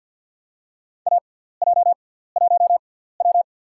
Morse code, heard as a keyed tone, about 50 wpm, spaced as for 17 wpm, 700 Hz. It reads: AJ1W
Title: USA Call Signs - 4 Characters - 6x Character Spacing - 50wpm